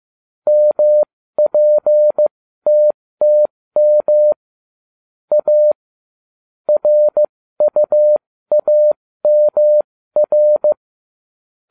Morse code, heard as a keyed tone, about 15 wpm, spaced as for 13 wpm, 600 Hz. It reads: MPTTM A RUAMR